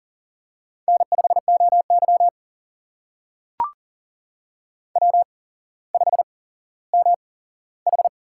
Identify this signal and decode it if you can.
Morse code, heard as a keyed tone, about 40 wpm, spaced as for 17 wpm, 700 Hz. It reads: N5OY W5MH
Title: USA Call Signs - 4 Characters - 8x Character Spacing - 40wpm